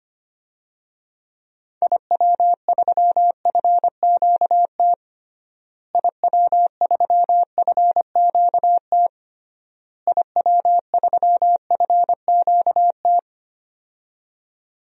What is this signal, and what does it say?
Morse code, heard as a keyed tone, about 25 wpm, 700 Hz. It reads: IW3FQT IW3FQT IW3FQT
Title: DXCC-100 Call Signs - 6 Characters - Repeated 3x - 25wpm